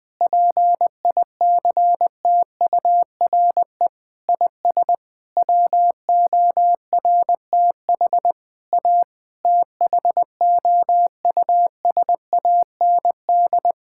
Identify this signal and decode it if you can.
Morse code, heard as a keyed tone, about 20 wpm, 700 Hz. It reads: PICTURE IS WORTH A THOUSAND
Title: Sentences with 7-Word Phrases - 20wpm